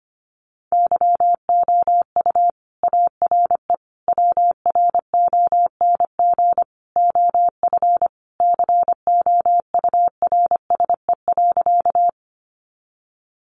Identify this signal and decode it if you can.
Morse code, heard as a keyed tone, about 25 wpm, 700 Hz. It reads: YOU ARE WRONG OF COURSE.
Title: Sentences from Top 900 Words 25wpm